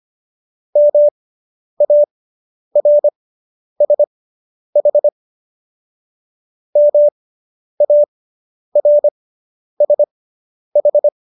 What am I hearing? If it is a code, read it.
Morse code, heard as a keyed tone, about 25 wpm, spaced as for 10 wpm, 590 Hz. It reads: MARSH MARSH